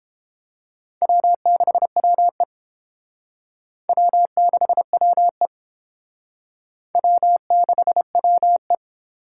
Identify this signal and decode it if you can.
Morse code, heard as a keyed tone, about 29 wpm, 700 Hz. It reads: W6WE W6WE W6WE